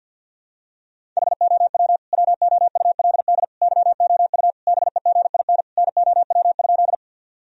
Morse code, heard as a keyed tone, about 46 wpm, 700 Hz. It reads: HOW WOULD YOU BEGIN NOW?
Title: Sentences from Top 1000 Words - Review Part 1 - 50wpm